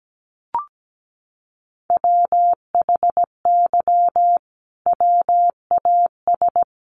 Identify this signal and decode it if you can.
Morse code, heard as a keyed tone, about 17 wpm, 700 Hz. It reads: WHY WAS
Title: Sentences from Top 1400 Words 17wpm